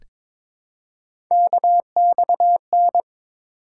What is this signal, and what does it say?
Morse code, heard as a keyed tone, about 22 wpm, 700 Hz. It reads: KXN
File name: practice / 22wpm / Three Letter-Number Combinations - 22wpm